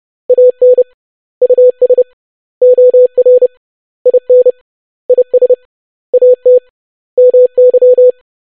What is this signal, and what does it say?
Morse code, heard as a keyed tone, about 30 wpm, 500 Hz. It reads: AN US OR IN IS AT MY
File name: basic100-25wpm